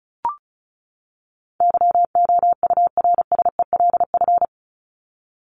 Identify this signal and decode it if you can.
Morse code, heard as a keyed tone, about 34 wpm, 700 Hz. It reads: YOURSELF
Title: Top 501-750 Words 35wpm